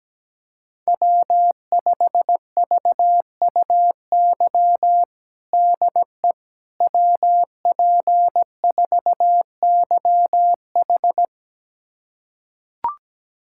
Morse code, heard as a keyed tone, about 17 wpm, 700 Hz. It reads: W5VUY DE WP4YH
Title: USA Callsign DE Callsign - 5 Characters - 17wpm